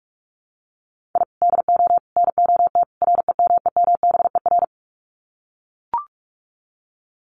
Morse code, heard as a keyed tone, about 44 wpm, 700 Hz. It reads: I DO NOT REMEMBER
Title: Sets of 4 Words 45wpm